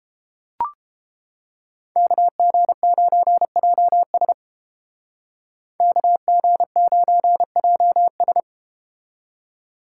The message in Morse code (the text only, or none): KG9JS KG9JS